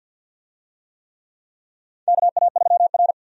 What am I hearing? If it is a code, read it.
Morse code, heard as a keyed tone, about 53 wpm, 700 Hz. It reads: KA3R
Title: USA Call Signs - 4 Characters - 50wpm